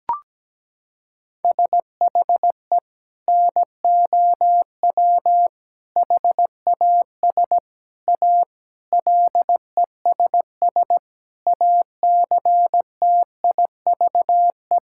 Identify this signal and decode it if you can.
Morse code, heard as a keyed tone, about 17 wpm, 700 Hz. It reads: SHE NOW HAS A LESS ACTIVE
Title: Sentences from Top 1000 Words - Review Part 6 - 17wpm